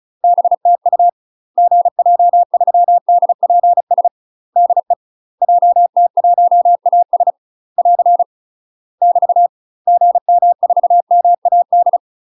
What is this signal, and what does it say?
Morse code, heard as a keyed tone, about 34 wpm, 700 Hz. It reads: BTU GJ3DPS DE JT1AS <AR> <BT> GM4MAD
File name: UK MASTER QSOs TRAINING/35WPM